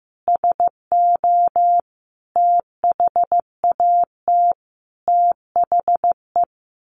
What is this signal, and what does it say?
Morse code, heard as a keyed tone, about 14 wpm, 700 Hz. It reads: SO THAT THE
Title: A Sound Of Thunder 15wpm